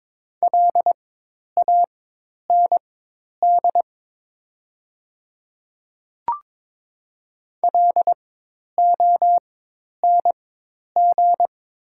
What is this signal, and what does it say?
Morse code, heard as a keyed tone, about 22 wpm, spaced as for 10 wpm, 700 Hz. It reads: LAND LONG
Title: 4x Character Spacing - Top 500 Words - 22wpm